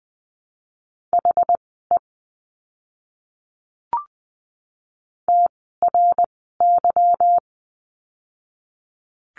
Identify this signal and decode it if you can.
Morse code, heard as a keyed tone, about 19 wpm, spaced as for 14 wpm, 700 Hz. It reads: HE TRY
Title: 160 Lesson 26f - Letter g - Words 20wpm